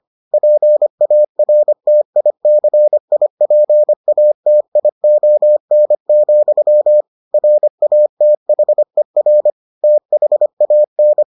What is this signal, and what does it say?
Morse code, heard as a keyed tone, about 25 wpm, 600 Hz. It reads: PARTICIPATION, RATHER THAN